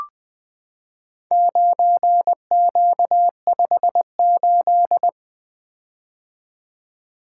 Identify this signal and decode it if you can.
Morse code, heard as a keyed tone, about 21 wpm, 700 Hz. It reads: 9Q58
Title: Four Letter-Number Combinations - 20wpm